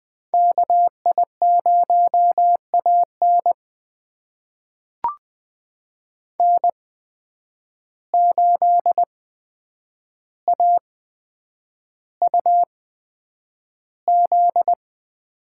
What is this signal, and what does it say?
Morse code, heard as a keyed tone, about 20 wpm, spaced as for 9 wpm, 700 Hz. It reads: KI0AN N8AUZ
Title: USA Call Signs - 8x Character Spacing - 20wpm